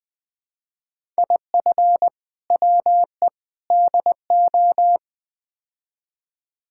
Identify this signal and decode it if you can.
Morse code, heard as a keyed tone, about 20 wpm, 700 Hz. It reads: IF WE DO
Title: Build A Phrase - 4-Word Phrases - 20wpm